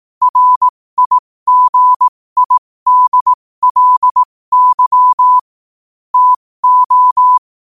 Morse code, heard as a keyed tone, about 18 wpm, spaced as for 15 wpm, 1000 Hz. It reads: RIGIDLY TO
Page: 15 WPM morse code quotes